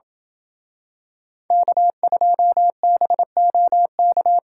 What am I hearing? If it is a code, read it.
Morse code, heard as a keyed tone, about 27 wpm, 700 Hz. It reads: K2BOK